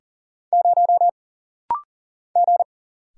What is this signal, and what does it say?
Morse code, heard as a keyed tone, about 40 wpm, 700 Hz. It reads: 0 G
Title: Single Letter-Number Alt - Rapid-Fire 40wpm